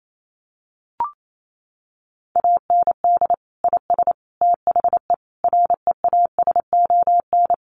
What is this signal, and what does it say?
Morse code, heard as a keyed tone, about 28 wpm, 700 Hz. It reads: AND IS THE REASON
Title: Sentences from Top 700 Words 28wpm